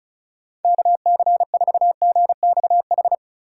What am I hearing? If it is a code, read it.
Morse code, heard as a keyed tone, about 45 wpm, 700 Hz. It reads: KC4GXH